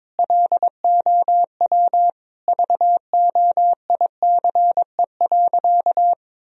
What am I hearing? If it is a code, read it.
Morse code, heard as a keyed tone, about 22 wpm, 700 Hz. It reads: LOW VOICE.
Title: Sentences from Top 1000 Words - Review Part 1 - 22wpm